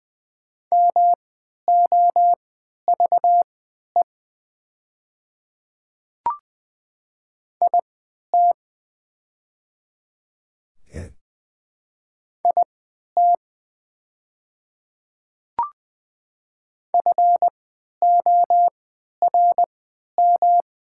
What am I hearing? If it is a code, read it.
Morse code, heard as a keyed tone, about 20 wpm, spaced as for 11 wpm, 700 Hz. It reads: MOVE IT IT FORM
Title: 3x Character Spacing - Top 200 Words - 20wpm